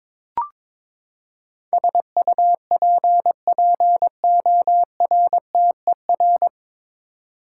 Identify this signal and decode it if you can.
Morse code, heard as a keyed tone, about 21 wpm, 700 Hz. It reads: SUPPORTER
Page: Morse Code Ninja practice 22wpm